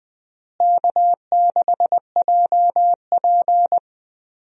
Morse code, heard as a keyed tone, about 20 wpm, 700 Hz. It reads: K6JP
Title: USA Call Signs - 4 Characters - Repeated 3x - 20wpm